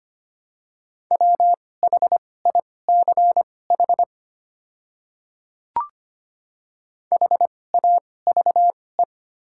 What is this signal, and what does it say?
Morse code, heard as a keyed tone, about 25 wpm, spaced as for 18 wpm, 700 Hz. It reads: WHICH HAVE
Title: 2x Character Spacing - Top 100 Words - 25wpm